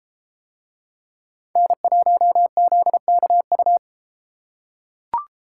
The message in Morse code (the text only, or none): N1ZKU